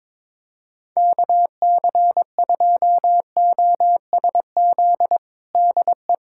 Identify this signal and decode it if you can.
Morse code, heard as a keyed tone, about 22 wpm, 700 Hz. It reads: KC2OSZ DE
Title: USA Callsign DE Callsign - 6 Characters - 22wpm